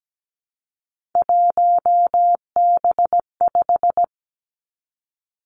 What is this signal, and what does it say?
Morse code, heard as a keyed tone, about 17 wpm, 700 Hz. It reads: 1B5
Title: Three Challenging Letter-Number Combinations 17wpm